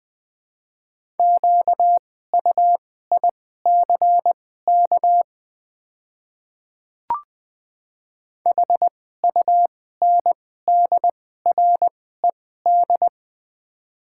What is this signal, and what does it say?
Morse code, heard as a keyed tone, about 21 wpm, spaced as for 15 wpm, 700 Hz. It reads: QUICK HUNDRED
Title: 2x Character Spacing - Top 500 Words - 20wpm